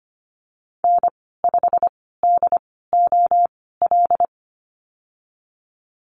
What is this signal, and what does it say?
Morse code, heard as a keyed tone, about 25 wpm, spaced as for 16 wpm, 700 Hz. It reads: N5DOL
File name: Lesson 15g - Number 5 - Call Signs 25wpm